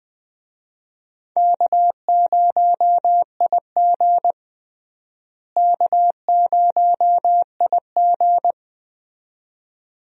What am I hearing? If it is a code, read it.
Morse code, heard as a keyed tone, about 20 wpm, 700 Hz. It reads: K0IG K0IG
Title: USA Call Signs - 4 Characters - Repeated 2x - 20wpm